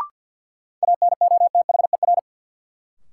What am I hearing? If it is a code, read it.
Morse code, heard as a keyed tone, about 52 wpm, 700 Hz. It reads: ANOTHER
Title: Sets of 5 Words Encore - New Word Drill - 50wpm